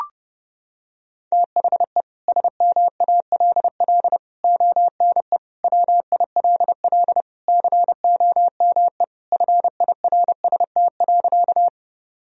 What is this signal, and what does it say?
Morse code, heard as a keyed tone, about 32 wpm, 700 Hz. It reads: THE SMALL ONE WILL COME FIRST.
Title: Sentences from Top 1000 Words - Review Part 1 - 30wpm